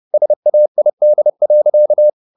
Morse code, heard as a keyed tone, about 30 wpm, 600 Hz. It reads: SAID.
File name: the old man and the sea / the old man and the sea 1 30wpm